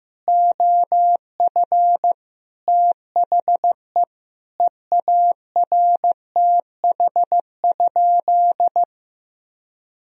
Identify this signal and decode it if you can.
Morse code, heard as a keyed tone, about 14 wpm, 700 Hz. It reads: OF THE EARTH?
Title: Sentences from Top 1000 Words - Review Part 3 - 15wpm